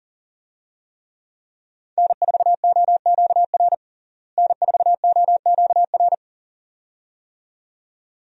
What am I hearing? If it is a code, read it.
Morse code, heard as a keyed tone, about 40 wpm, 700 Hz. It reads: N4OQR N4OQR
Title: USA Call Signs - 5 Characters - Repeated 2x - 40wpm